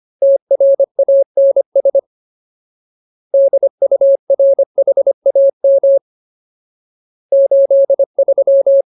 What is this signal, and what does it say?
Morse code, heard as a keyed tone, about 25 wpm, 550 Hz. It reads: TRANS DURHAM 83
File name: Ultimate Word List / 25WPM Ultimate